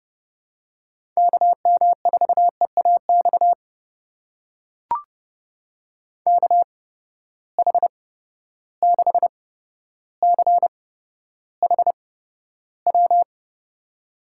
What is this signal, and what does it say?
Morse code, heard as a keyed tone, about 32 wpm, spaced as for 13 wpm, 700 Hz. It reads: KM4EAX KH6CHW
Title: USA Call Signs - 6 Characters - 8x Character Spacing - 30wpm